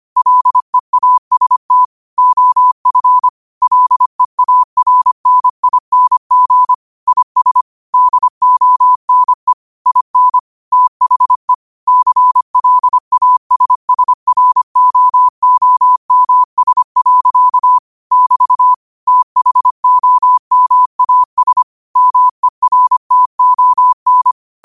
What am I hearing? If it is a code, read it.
Morse code, heard as a keyed tone, about 25 wpm, 1000 Hz. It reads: LEAST OF LEARNING IS DONE IN THE CLASSROOMS. <BT> THOMAS MERTON
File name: qotdfiles / 25wpm